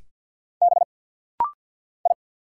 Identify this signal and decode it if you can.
Morse code, heard as a keyed tone, about 50 wpm, 700 Hz. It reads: B I
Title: Single Letters - Rapid-Fire 50wpm